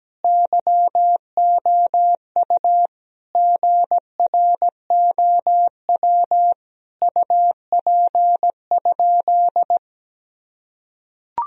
Morse code, heard as a keyed tone, about 17 wpm, 700 Hz. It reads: YOU GROW UP?
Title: Sentences from Top 1000 Words - Review Part 1 - 17wpm